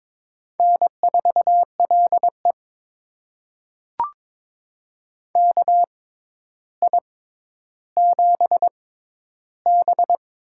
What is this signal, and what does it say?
Morse code, heard as a keyed tone, about 22 wpm, spaced as for 11 wpm, 700 Hz. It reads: N4LE KI7B
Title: USA Call Signs - 4 Characters - 6x Character Spacing - 22wpm